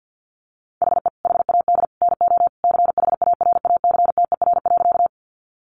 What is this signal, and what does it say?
Morse code, heard as a keyed tone, about 50 wpm, 700 Hz. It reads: HE HAD NO CHARACTER.
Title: Sentences from Top 1200 Words 50wpm